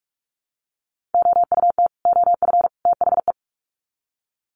Morse code, heard as a keyed tone, about 44 wpm, 700 Hz. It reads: OUT OF THE
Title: Sets of 3 Words 45wpm